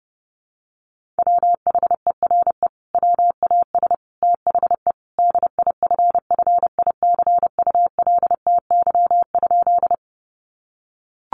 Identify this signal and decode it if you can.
Morse code, heard as a keyed tone, about 30 wpm, 700 Hz. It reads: WHERE WAS THE DIFFICULTY?
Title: Sentences from Top 1500 Words 30wpm